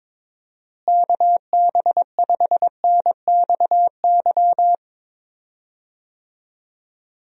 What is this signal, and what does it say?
Morse code, heard as a keyed tone, about 22 wpm, 700 Hz. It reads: KB5NXY
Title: USA Call Signs - 6 Characters - Repeated 3x - 22wpm